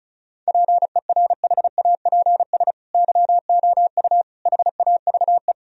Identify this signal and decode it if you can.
Morse code, heard as a keyed tone, about 34 wpm, 700 Hz. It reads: PERHAPS YOU HAVE
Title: Sentences from Top 1000 Words - Review Part 1 - 35wpm